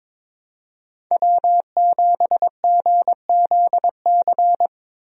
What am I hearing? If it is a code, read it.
Morse code, heard as a keyed tone, about 22 wpm, 700 Hz. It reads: W7GZC
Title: USA Call Signs - 5 Characters - Repeated 2x - 22wpm